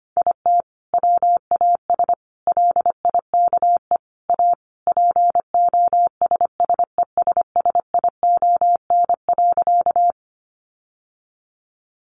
Morse code, heard as a keyed tone, about 25 wpm, 700 Hz. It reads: IT WAS LIKE A POSSESSION.
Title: Sentences from Top 1400 Words 25wpm